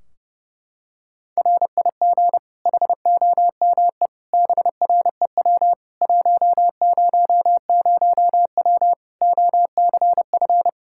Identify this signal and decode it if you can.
Morse code, heard as a keyed tone, about 32 wpm, 700 Hz. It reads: RIG HOME BREW 100W OCF
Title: K1USN SST QSO Practice 30wpm